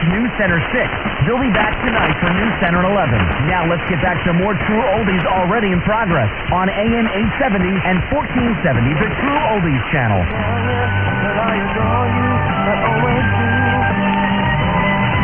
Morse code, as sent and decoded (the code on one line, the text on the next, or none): none